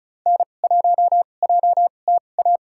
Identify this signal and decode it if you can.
Morse code, heard as a keyed tone, about 34 wpm, spaced as for 25 wpm, 700 Hz. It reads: N1JTA
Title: USA Call Signs - 5 Characters - 2x Character Spacing - 35wpm